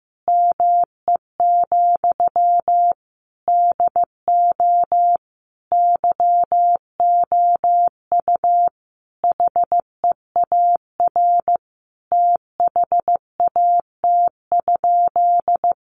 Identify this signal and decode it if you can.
Morse code, heard as a keyed tone, about 15 wpm, 700 Hz. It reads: ME, DO YOU HEAR THAT?
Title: Sentences from Top 1500 Words 15wpm